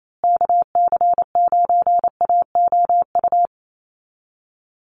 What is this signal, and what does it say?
Morse code, heard as a keyed tone, about 28 wpm, 700 Hz. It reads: KC9AOU
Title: USA Call Signs Repeated 3x 28wpm